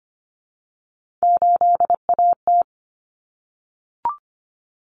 Morse code, heard as a keyed tone, about 25 wpm, 700 Hz. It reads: 8AT
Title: Common Cut Numbers - 3-Digit 25wpm